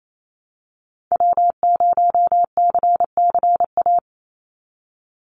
Morse code, heard as a keyed tone, about 28 wpm, 700 Hz. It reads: W0CCA
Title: NA SOTA Activator Callsigns 28wpm